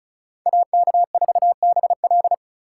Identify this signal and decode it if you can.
Morse code, heard as a keyed tone, about 35 wpm, 690 Hz. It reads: AK4BL
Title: USA Call Signs - 5 Characters - 2x Character Spacing - 35wpm